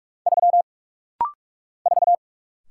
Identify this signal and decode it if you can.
Morse code, heard as a keyed tone, about 47 wpm, 700 Hz. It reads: 3 4
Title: Single Letter-Number - Rapid-Fire 45wpm